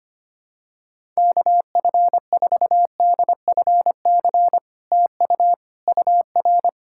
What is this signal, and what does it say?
Morse code, heard as a keyed tone, about 25 wpm, 700 Hz. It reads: KF4DFC TU UR
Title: SOTA QSO Practice 25wpm